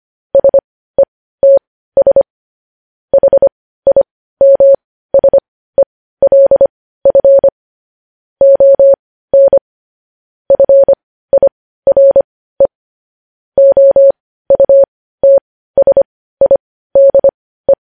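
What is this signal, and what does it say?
Morse code, heard as a keyed tone, about 25 wpm, spaced as for 15 wpm, 550 Hz. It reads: SETS HIMSELF ON FIRE OUTSIDE
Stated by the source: Morse code news headlines at 15.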